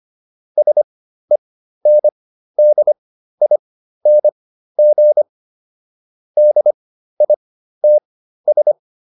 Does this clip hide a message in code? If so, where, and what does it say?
Yes: Morse code from 0.6 to 8.7 s, heard as a keyed tone, about 25 wpm, spaced as for 13 wpm, 600 Hz. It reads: SENDING DITS